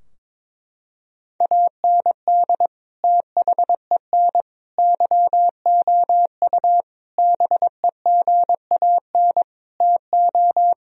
Morse code, heard as a keyed tone, about 22 wpm, 700 Hz. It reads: AND THEN YOU BEGAN TO
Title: Sentences from Top 1000 Words - Review Part 2 - 22wpm